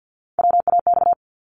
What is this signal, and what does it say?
Morse code, heard as a keyed tone, about 50 wpm, 700 Hz. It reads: WAX
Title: Multi-letter Phonemes 50wpm